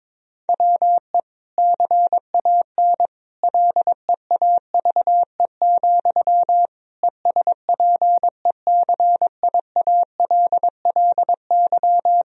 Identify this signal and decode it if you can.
Morse code, heard as a keyed tone, about 22 wpm, 700 Hz. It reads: WE CAN LEAVE, ESPECIALLY
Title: Sentences from Top 1000 Words - Review Part 4 - 22wpm